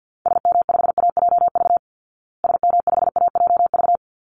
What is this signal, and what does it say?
Morse code, heard as a keyed tone, about 48 wpm, 700 Hz. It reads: SM5AJV SM5AJV
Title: Call Signs Repeated 3x - 50wpm